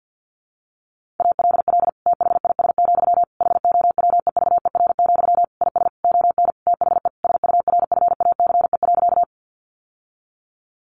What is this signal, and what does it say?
Morse code, heard as a keyed tone, about 50 wpm, 700 Hz. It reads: ALL THIS, HOWEVER, IS ON THE SURFACE.
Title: Sentences from Top 600 Words 50wpm